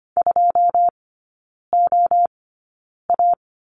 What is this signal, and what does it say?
Morse code, heard as a keyed tone, about 25 wpm, spaced as for 9 wpm, 690 Hz. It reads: 2OA
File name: Lesson 14e - Number 2 - Three Characters 25wpm